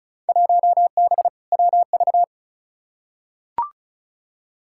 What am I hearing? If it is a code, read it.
Morse code, heard as a keyed tone, about 36 wpm, 700 Hz. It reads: 1B WV
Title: Field Day Exchanges Encore - 35wpm